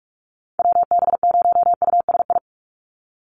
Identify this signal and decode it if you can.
Morse code, heard as a keyed tone, about 44 wpm, 700 Hz. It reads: WB0USI
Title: NA SOTA Chaser Callsigns 45wpm